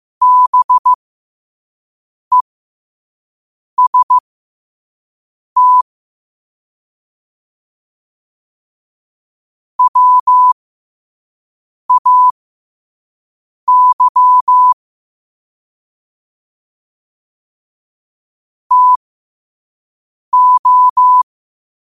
Morse code, heard as a keyed tone, about 15 wpm, spaced as for 5 wpm, 1000 Hz. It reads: BEST WAY TO
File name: qotdfiles / 5wpm